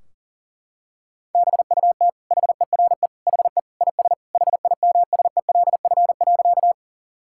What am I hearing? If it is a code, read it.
Morse code, heard as a keyed tone, about 40 wpm, 700 Hz. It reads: BUT HERE HE IS HIMSELF.
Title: Sentences from Top 1600 Words 40wpm